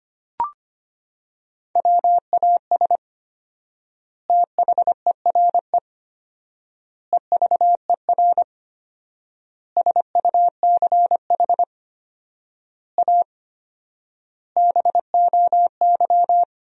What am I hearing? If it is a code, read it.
Morse code, heard as a keyed tone, about 25 wpm, 700 Hz. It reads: WAS THERE EVER SUCH A BOY
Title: 4x Word-Spacing - Sentences from Top 400 Words - 25wpm